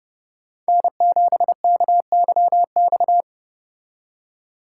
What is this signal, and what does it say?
Morse code, heard as a keyed tone, about 30 wpm, 700 Hz. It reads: N7KYX